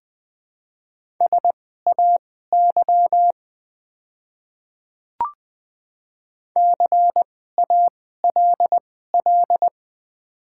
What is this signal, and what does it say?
Morse code, heard as a keyed tone, about 20 wpm, spaced as for 14 wpm, 700 Hz. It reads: SAY CALL
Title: 2x Character Spacing - Top 200 Words - 20wpm